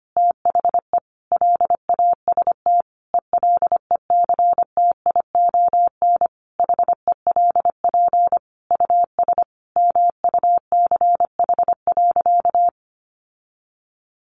Morse code, heard as a keyed tone, about 25 wpm, 700 Hz. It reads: THE LAST ELECTION HELP US MUCH.
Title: Sentences from Top 1500 Words 25wpm